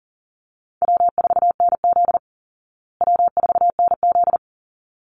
Morse code, heard as a keyed tone, about 40 wpm, 700 Hz. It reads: W4NZ W4NZ
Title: Easy Call Signs Repeated 2x - 40wpm